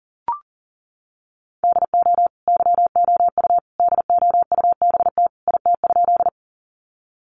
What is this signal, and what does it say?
Morse code, heard as a keyed tone, about 40 wpm, 700 Hz. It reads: DO YOU DOUBT IT?
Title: Sentences from Top 1500 Words 40wpm